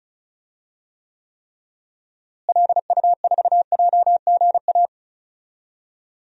Morse code, heard as a keyed tone, about 34 wpm, 700 Hz. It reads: LU4JGA
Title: DXCC-100 Call Signs - 6 Characters - Repeated 3x - 35wpm